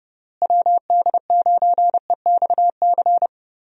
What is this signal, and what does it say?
Morse code, heard as a keyed tone, about 31 wpm, 700 Hz. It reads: WD9EXC
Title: USA Call Signs - 6 Characters - Repeated 3x - 30wpm